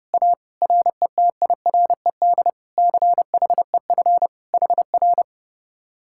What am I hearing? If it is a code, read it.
Morse code, heard as a keyed tone, about 38 wpm, 700 Hz. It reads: A RETIRED CHEF HR